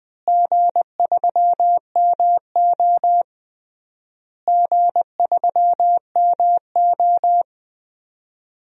Morse code, heard as a keyed tone, about 19 wpm, 700 Hz. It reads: G3MO G3MO
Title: DXCC-100 Call Signs - 4 Characters - Repeated 3x - 20wpm